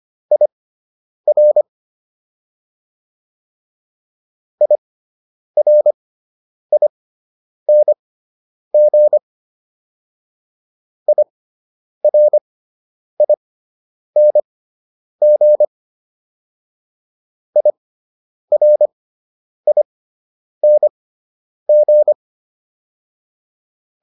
Morse code, heard as a keyed tone, about 25 wpm, spaced as for 9 wpm, 600 Hz. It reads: IR IRING IRING IRING